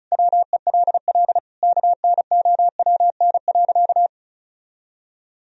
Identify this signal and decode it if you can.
Morse code, heard as a keyed tone, about 36 wpm, 700 Hz. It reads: WELL KNOWN.
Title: Sentences from Top 1000 Words - Review Part 4 - 35wpm